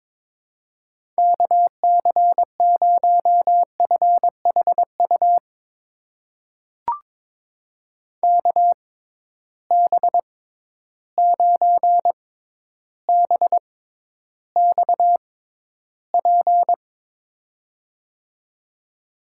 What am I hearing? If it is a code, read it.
Morse code, heard as a keyed tone, about 22 wpm, spaced as for 11 wpm, 700 Hz. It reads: KC0FHU KB9BXP
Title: USA Call Signs - 6 Characters - 6x Character Spacing - 22wpm